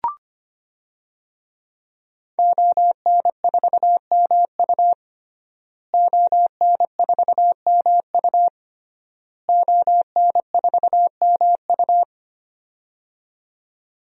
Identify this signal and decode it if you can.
Morse code, heard as a keyed tone, about 25 wpm, 700 Hz. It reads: ON4MU ON4MU ON4MU
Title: DXCC-100 Call Signs - 5 Characters - Repeated 3x - 25wpm